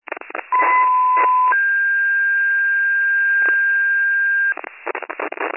Two-tone page, heard as a one-second tone, then a three-second tone, about 1010 then 1600 Hz, with noise. It is heard from 0.5 to 4.5 s.